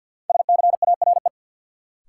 Morse code, heard as a keyed tone, about 51 wpm, 700 Hz. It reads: SCARE
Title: Sentences from Top 1000 Words - Review Part 5 - New Word Drill - 50wpm